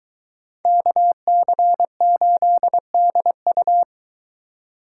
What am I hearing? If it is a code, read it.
Morse code, heard as a keyed tone, about 23 wpm, 700 Hz. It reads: KC8DU